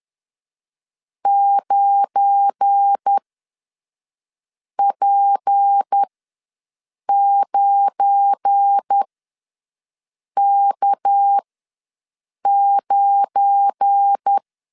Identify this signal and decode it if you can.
Morse code, heard as a keyed tone, about 10 wpm, spaced as for 5 wpm, 800 Hz. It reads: 9P9K9
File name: LEZIONI / 7 GF9Z / 7 GF9 ws 10wpm 4034